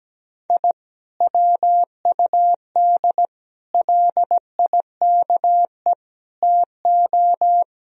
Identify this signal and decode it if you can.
Morse code, heard as a keyed tone, about 17 wpm, 700 Hz. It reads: I WUD LIKE TO
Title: Sentences from Top 200 Words and Abbreviations 17wpm